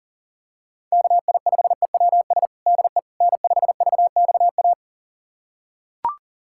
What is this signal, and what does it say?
Morse code, heard as a keyed tone, about 40 wpm, 700 Hz. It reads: KI5EWS DE N5VXA
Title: USA Callsign DE Callsign 40wpm